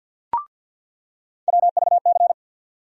Morse code, heard as a keyed tone, about 48 wpm, 700 Hz. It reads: WVC
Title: USA Call Sign Postfixes - 3 Letters 50wpm